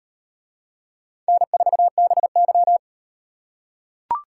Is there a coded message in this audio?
Yes: Morse code from 1.3 to 2.8 s, heard as a keyed tone, about 38 wpm, 700 Hz. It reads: N4BY